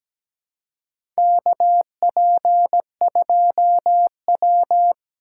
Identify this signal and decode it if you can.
Morse code, heard as a keyed tone, about 17 wpm, 700 Hz. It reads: KP2W
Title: USA Call Signs - 4 Characters - 3x Character Spacing - 17wpm